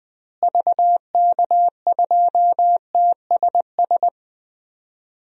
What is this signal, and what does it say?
Morse code, heard as a keyed tone, about 20 wpm, 700 Hz. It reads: VK2TSS
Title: DXCC-100 Call Signs - 6 Characters - Repeated 2x - 20wpm